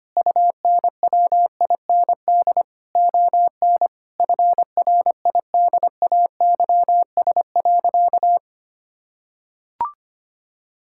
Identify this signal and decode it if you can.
Morse code, heard as a keyed tone, about 25 wpm, 700 Hz. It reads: UNWIND ON FRIDAYS.
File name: practice / 25wpm / Difficult Sentences with Hints - 25wpm